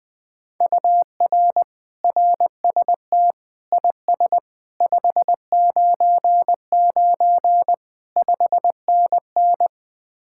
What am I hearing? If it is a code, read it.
Morse code, heard as a keyed tone, about 20 wpm, 700 Hz. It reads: UR RST IS 599 5NN
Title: Speed-Racing Common QSO Phrases - 17wpm